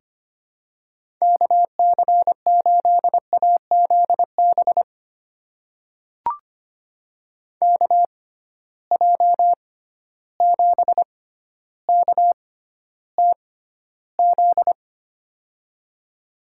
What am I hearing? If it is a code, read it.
Morse code, heard as a keyed tone, about 25 wpm, spaced as for 13 wpm, 700 Hz. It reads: KC8AZB KJ7KTZ
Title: USA Call Signs - 6x Character Spacing - 25wpm